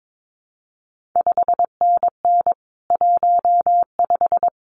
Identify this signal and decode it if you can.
Morse code, heard as a keyed tone, about 22 wpm, 700 Hz. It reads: 5NN 15
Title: CQ WW DX Contest Exchange - 22wpm